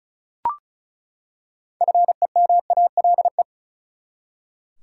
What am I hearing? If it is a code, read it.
Morse code, heard as a keyed tone, about 36 wpm, 700 Hz. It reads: FEMALE
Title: Six Letter Words - 35wpm